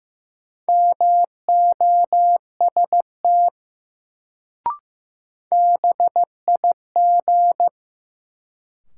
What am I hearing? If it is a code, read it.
Morse code, heard as a keyed tone, about 15 wpm, 700 Hz. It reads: MOST BIG
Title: Sentences from Top 200 Words - New Word Drill - 15wpm